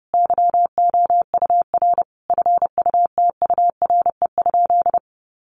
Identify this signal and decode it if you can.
Morse code, heard as a keyed tone, about 30 wpm, 700 Hz. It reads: YOUR FUTURE?
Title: Sentences from Top 700 Words 30wpm